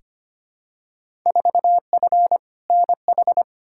Morse code, heard as a keyed tone, about 25 wpm, 700 Hz. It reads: 4F NH
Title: Field Day Exchanges Encore - 25wpm